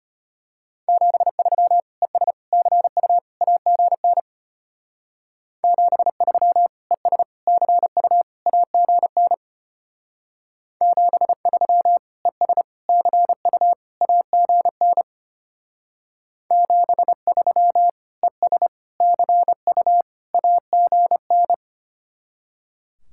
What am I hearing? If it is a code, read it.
Morse code, heard as a keyed tone, about 32 wpm, 700 Hz. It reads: 73 ES CU AGN 73 ES CU AGN 73 ES CU AGN 73 ES CU AGN